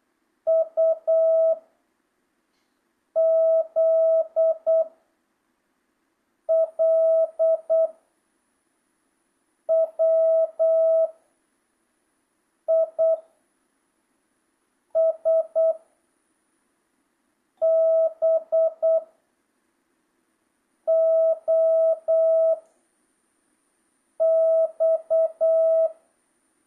Morse code, heard as a keyed tone, about 8 wpm, 660 Hz. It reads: U Z L W I S B O X
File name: jh8cbh / 2025cw